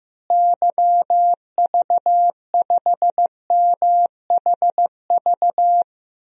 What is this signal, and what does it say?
Morse code, heard as a keyed tone, about 15 wpm, 700 Hz. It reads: YV5MHV
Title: DXCC-100 Call Signs - 6 Characters - Repeated 3x - 15wpm